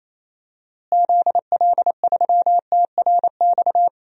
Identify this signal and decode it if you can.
Morse code, heard as a keyed tone, about 28 wpm, 700 Hz. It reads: ZL3TRX
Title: DXCC-100 Call Signs - 6 Characters - Repeated 3x - 28wpm